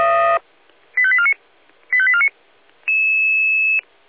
Five-tone selective call (ZVEI), heard as quick runs of five tones, tones of about 70 ms. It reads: 85849 85849
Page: SDR/ZVEI